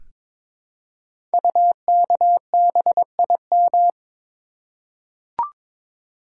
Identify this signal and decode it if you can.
Morse code, heard as a keyed tone, about 22 wpm, 700 Hz. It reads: UKBIM
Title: Code Groups - 5 Letters - 22wpm